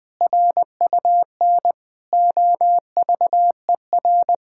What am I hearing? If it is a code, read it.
Morse code, heard as a keyed tone, about 20 wpm, 700 Hz. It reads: RUN OVER
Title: Pangram Challenge v2 - 20wpm